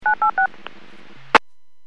DTMF, heard as telephone touch tones, tones of about 110 ms. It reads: 986
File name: WPS / 02117 11-42 WPS 451.25 WPS